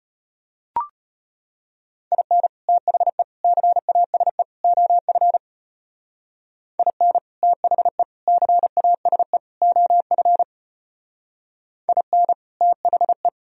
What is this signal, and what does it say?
Morse code, heard as a keyed tone, about 33 wpm, 700 Hz. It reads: IN THE CASE OF IN THE CASE OF IN THE